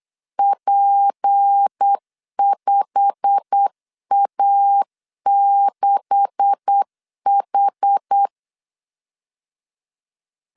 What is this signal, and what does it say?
Morse code, heard as a keyed tone, about 8 wpm, 800 Hz. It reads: P5A6H
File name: CW / DET / 1 9 1500C S C 8wpm 5653